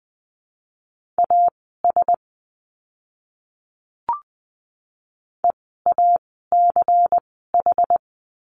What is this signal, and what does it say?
Morse code, heard as a keyed tone, about 20 wpm, spaced as for 15 wpm, 700 Hz. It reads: AS EACH
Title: 192 Lesson 31f - Letter v - Words 20wpm